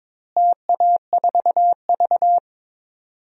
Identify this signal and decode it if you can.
Morse code, heard as a keyed tone, about 21 wpm, 700 Hz. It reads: TA4V
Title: DXCC-100 Call Signs - 4 Characters - Repeated 3x - 22wpm